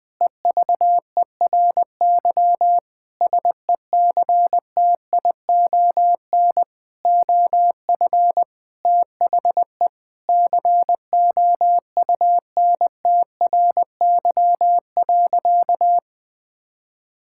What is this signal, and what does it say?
Morse code, heard as a keyed tone, about 20 wpm, 700 Hz. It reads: EVERY SECTION OF THE COUNTRY.
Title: Sentences from Top 1000 Words - Review Part 5 - 20wpm